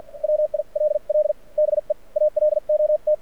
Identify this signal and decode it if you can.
Morse code, heard as a keyed tone, about 53 wpm, spaced as for 43 wpm, 600 Hz. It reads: WILL BE ALOT
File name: archived / CRSnet